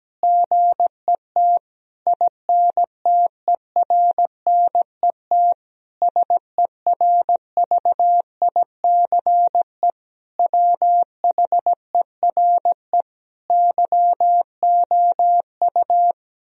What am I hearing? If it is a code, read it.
Morse code, heard as a keyed tone, about 17 wpm, 700 Hz. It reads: GET INTERNET SERVICE WHERE YOU
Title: Sentences from Top 1000 Words - Review Part 4 - 17wpm